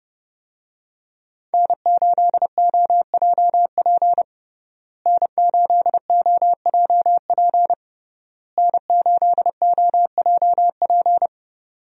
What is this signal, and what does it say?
Morse code, heard as a keyed tone, about 32 wpm, 700 Hz. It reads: N8OJP N8OJP N8OJP
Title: USA Call Signs - 5 Characters - Repeated 3x - 30wpm